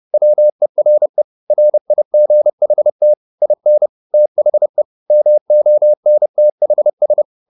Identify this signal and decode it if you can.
Morse code, heard as a keyed tone, about 30 wpm, 600 Hz. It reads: WERE RIGHT IN THE MONTHS
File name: the old man and the sea / the old man and the sea 1 30wpm